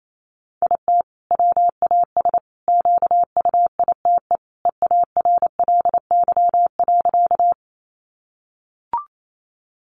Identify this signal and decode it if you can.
Morse code, heard as a keyed tone, about 28 wpm, 700 Hz. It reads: IT WAS QUITE EARLY.
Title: Sentences from Top 700 Words 28wpm